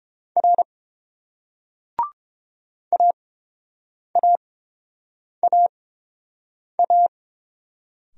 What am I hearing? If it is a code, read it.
Morse code, heard as a keyed tone, about 30 wpm, spaced as for 7 wpm, 700 Hz. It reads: R AAAA